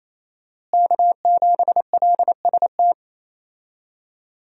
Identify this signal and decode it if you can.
Morse code, heard as a keyed tone, about 28 wpm, 700 Hz. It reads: K7LST
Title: USA Call Signs - 5 Characters - Repeated 2x - 28wpm